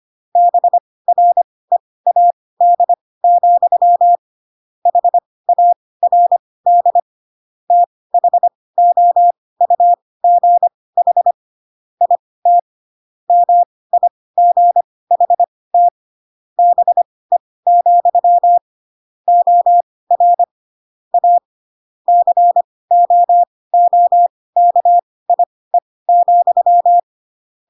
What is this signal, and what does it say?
Morse code, heard as a keyed tone, about 25 wpm, spaced as for 18 wpm, 700 Hz. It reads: BREAD, HARD THOUGH IT MIGHT BE, OR A COOKIE,